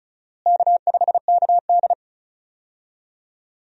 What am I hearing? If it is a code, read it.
Morse code, heard as a keyed tone, about 36 wpm, 700 Hz. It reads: K5KD
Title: USA Call Signs - 4 Characters - Repeated 2x - 35wpm